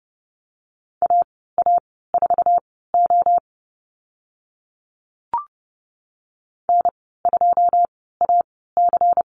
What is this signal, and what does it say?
Morse code, heard as a keyed tone, about 30 wpm, spaced as for 17 wpm, 700 Hz. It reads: AA4O N2AC
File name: Lesson 19g - Letter w - Call Signs 30wpm